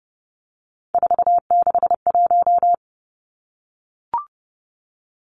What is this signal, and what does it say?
Morse code, heard as a keyed tone, about 30 wpm, 700 Hz. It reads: 461
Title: Three Challenging Letter-Number Combinations 30wpm